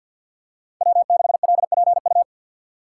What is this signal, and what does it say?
Morse code, heard as a keyed tone, about 52 wpm, 700 Hz. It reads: W6LPU
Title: USA Call Signs - 3x Character Spacing - 50wpm